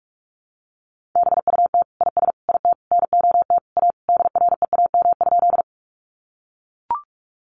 Morse code, heard as a keyed tone, about 44 wpm, 700 Hz. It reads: BUT IS IT NOT A DREAM?
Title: Sentences from Top 1500 Words 45wpm